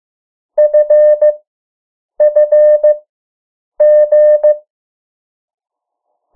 Morse code, heard as a keyed tone, about 15 wpm, 600 Hz. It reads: F F G
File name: morse mp3 / 03 urok 2